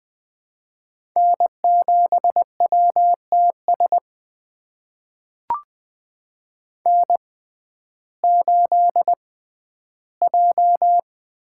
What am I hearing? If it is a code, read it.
Morse code, heard as a keyed tone, about 20 wpm, spaced as for 12 wpm, 700 Hz. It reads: N7WTS N8J
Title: USA Call Signs - 5 Characters - 6x Character Spacing - 20wpm